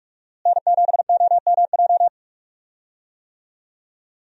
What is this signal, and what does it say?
Morse code, heard as a keyed tone, about 46 wpm, 700 Hz. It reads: N7OMJ
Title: USA Call Signs - 5 Characters - Repeated 2x - 45wpm